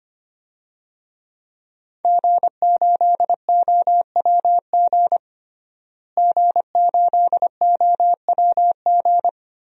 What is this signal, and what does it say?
Morse code, heard as a keyed tone, about 25 wpm, 700 Hz. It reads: G8OWG G8OWG
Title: DXCC-100 Call Signs - 5 Characters - Repeated 3x - 25wpm